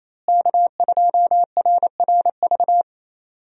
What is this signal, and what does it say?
Morse code, heard as a keyed tone, about 28 wpm, 700 Hz. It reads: K2RRV
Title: USA Call Signs - 5 Characters - Repeated 3x - 28wpm